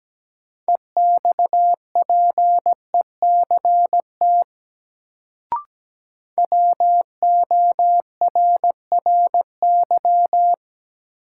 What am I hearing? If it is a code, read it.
Morse code, heard as a keyed tone, about 17 wpm, 700 Hz. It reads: EXPECT WORRY
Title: Sentences from Top 1100 Words - New Word Drill - 17wpm